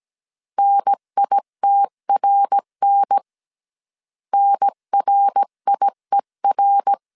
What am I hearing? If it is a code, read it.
Morse code, heard as a keyed tone, about 17 wpm, 800 Hz. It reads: NITRN NRIER
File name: CW / DET / 1 3 1500C S A 16wpm 2141